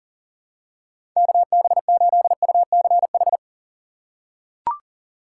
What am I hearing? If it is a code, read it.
Morse code, heard as a keyed tone, about 40 wpm, 700 Hz. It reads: KB8UCH
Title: USA Call Signs - 6 Characters - 3x Character Spacing - 40wpm